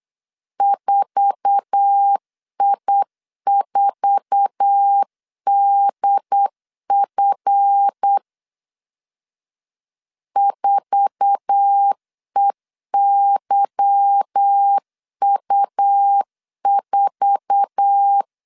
Morse code, heard as a keyed tone, about 8 wpm, 800 Hz. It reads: 4I4DF 4EYU4
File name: LEZIONI / 8 YJ45 / 8 YJ4 8wpm 3246